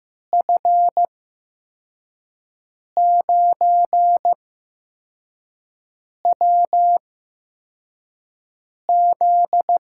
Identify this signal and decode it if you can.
Morse code, heard as a keyed tone, about 15 wpm, spaced as for 4 wpm, 700 Hz. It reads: F9WZ
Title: USA Call Signs - 5 Characters - 8x Character Spacing - 15wpm